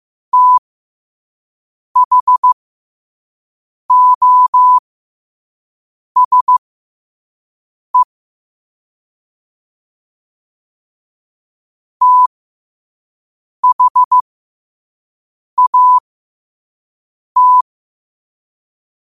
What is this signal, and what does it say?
Morse code, heard as a keyed tone, about 15 wpm, spaced as for 5 wpm, 1000 Hz. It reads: THOSE THAT